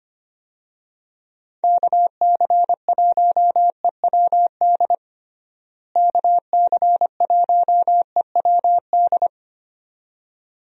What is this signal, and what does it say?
Morse code, heard as a keyed tone, about 25 wpm, 700 Hz. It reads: KC1EWD KC1EWD
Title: USA Call Signs - 6 Characters - Repeated 2x - 25wpm